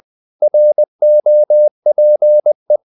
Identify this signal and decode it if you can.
Morse code, heard as a keyed tone, about 20 wpm, 600 Hz. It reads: ROPE
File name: the old man and the sea part 2 20wpm